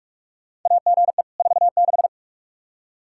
Morse code, heard as a keyed tone, about 47 wpm, 700 Hz. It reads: AGE 46
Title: Practice QSO - Exchange 3 - More Info - 45wpm